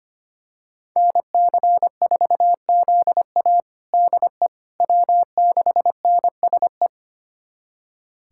Morse code, heard as a keyed tone, about 25 wpm, 700 Hz. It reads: NC4ZA DE W6NSE
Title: USA Callsign DE Callsign - 5 Characters - 25wpm